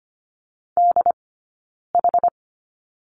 Morse code, heard as a keyed tone, about 25 wpm, 700 Hz. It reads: D H
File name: Lesson 17d - Letter u - Two Characters 25wpm